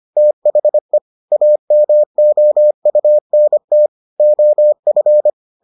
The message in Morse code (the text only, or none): THE AMOUNT OF